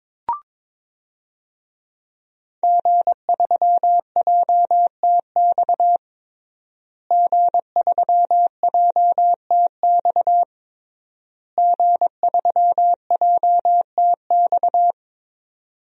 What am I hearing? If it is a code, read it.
Morse code, heard as a keyed tone, about 22 wpm, 700 Hz. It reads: G3JTX G3JTX G3JTX
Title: DXCC-100 Call Signs - 5 Characters - Repeated 3x - 22wpm